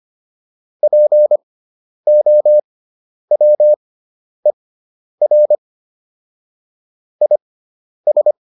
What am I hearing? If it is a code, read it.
Morse code, heard as a keyed tone, about 25 wpm, spaced as for 10 wpm, 600 Hz. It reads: POWER IS